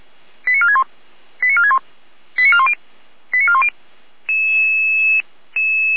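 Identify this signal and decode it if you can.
Five-tone selective call (ZVEI), heard as quick runs of five tones, tones of about 70 ms. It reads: 89461 89461 89410 89410
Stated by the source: Five-tone ZVEI selective call received on an SDR.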